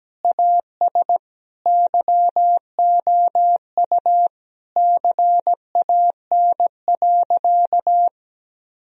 Morse code, heard as a keyed tone, about 17 wpm, 700 Hz. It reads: AS YOU CAN.
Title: Sentences from Top 1000 Words - Review Part 1 - 17wpm